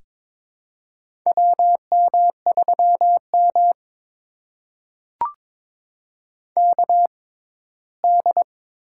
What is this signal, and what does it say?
Morse code, heard as a keyed tone, about 22 wpm, spaced as for 15 wpm, 700 Hz. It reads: WM3M KD
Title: USA Call Signs - 6x Character Spacing - 22wpm